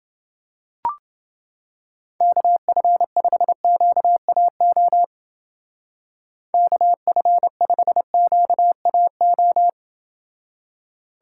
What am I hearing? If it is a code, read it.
Morse code, heard as a keyed tone, about 29 wpm, 700 Hz. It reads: KF5QAO KF5QAO